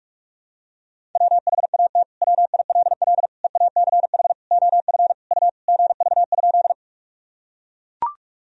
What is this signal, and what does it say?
Morse code, heard as a keyed tone, about 41 wpm, 700 Hz. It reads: WHAT WILL EACH OF U GV?
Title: Sentences from Top 200 Words and Abbreviations 45wpm